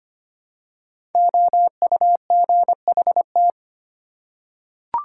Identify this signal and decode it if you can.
Morse code, heard as a keyed tone, about 25 wpm, 700 Hz. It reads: OUGHT
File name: 5 Letter Words - First 3 Letters Given - 25wpm